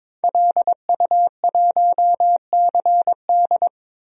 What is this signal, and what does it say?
Morse code, heard as a keyed tone, about 22 wpm, 700 Hz. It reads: LU1CD
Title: DXCC-100 Call Signs - 5 Characters - 22wpm